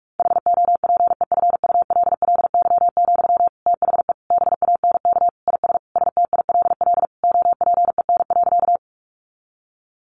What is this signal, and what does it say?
Morse code, heard as a keyed tone, about 44 wpm, 700 Hz. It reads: HOPEFULLY, THE BANK IS STILL OPEN.
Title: Sentences from Top 1500 Words 45wpm